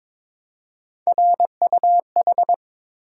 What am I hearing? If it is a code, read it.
Morse code, heard as a keyed tone, about 22 wpm, 700 Hz. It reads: RUH
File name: practice / 22wpm / Three Letter-Number Combinations - 22wpm